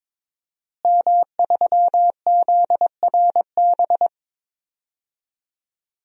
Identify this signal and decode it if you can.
Morse code, heard as a keyed tone, about 21 wpm, 700 Hz. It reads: M3ZRB
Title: DXCC-100 Call Signs - 5 Characters - Repeated 3x - 22wpm